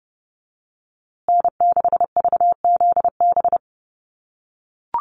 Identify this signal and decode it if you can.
Morse code, heard as a keyed tone, about 29 wpm, 700 Hz. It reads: N6VZB
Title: USA Call Signs Repeated 3x 30wpm